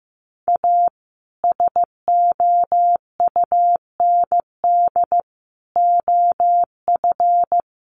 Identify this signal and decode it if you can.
Morse code, heard as a keyed tone, about 15 wpm, 700 Hz. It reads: A SOUND OF
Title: A Sound Of Thunder 15wpm